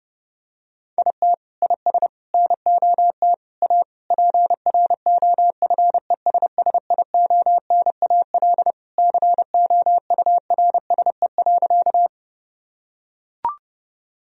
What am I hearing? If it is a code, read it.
Morse code, heard as a keyed tone, about 30 wpm, 700 Hz. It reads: IT IS NOT A PROFESSIONAL COURSE.
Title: Sentences from Top 1000 Words - Review Part 4 - 30wpm